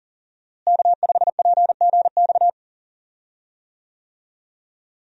Morse code, heard as a keyed tone, about 40 wpm, 700 Hz. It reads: K5PGX
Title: USA Call Signs - 5 Characters - Repeated 2x - 40wpm